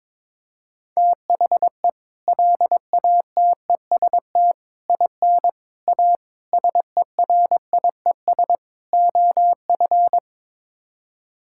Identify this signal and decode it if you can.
Morse code, heard as a keyed tone, about 22 wpm, 700 Hz. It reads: THE LATEST IN A SERIES OF
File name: practice / 22wpm / Sets of 6 Words - 22wpm